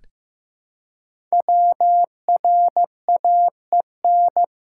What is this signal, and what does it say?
Morse code, heard as a keyed tone, about 15 wpm, 700 Hz. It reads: WRAEN
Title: Code Groups - 5 Letters - 15wpm